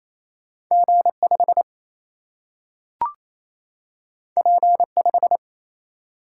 Morse code, heard as a keyed tone, about 28 wpm, 700 Hz. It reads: G5 P5
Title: Single-Letter Single-Number 28wpm